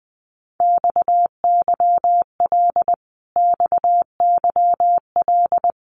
Morse code, heard as a keyed tone, about 20 wpm, 700 Hz. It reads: XYL XYL
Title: Abbreviations Repeated 3x - 20wpm